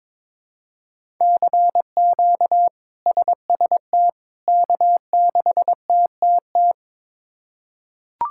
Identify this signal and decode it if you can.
Morse code, heard as a keyed tone, about 21 wpm, 700 Hz. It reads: CQ SST K6TTT
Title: K1USN SST QSO Practice - 22wpm